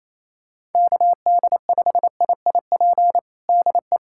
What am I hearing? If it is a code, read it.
Morse code, heard as a keyed tone, about 28 wpm, 700 Hz. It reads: KD5IIP DE
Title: USA Callsign DE Callsign - 6 Characters - 28wpm